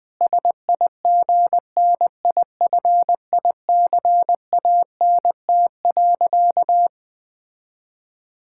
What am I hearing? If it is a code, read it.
Morse code, heard as a keyed tone, about 20 wpm, 700 Hz. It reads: SIGNIFICANT.
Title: Sentences from Top 1000 Words - Review Part 6 - 20wpm